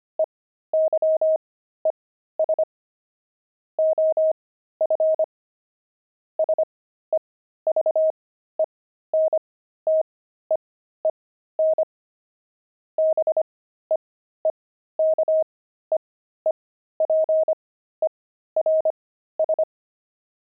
Morse code, heard as a keyed tone, about 25 wpm, spaced as for 13 wpm, 630 Hz. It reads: EYES OF SEVENTEEN BEEKEEPERS
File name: pangram-fs-13-cs-25-beehive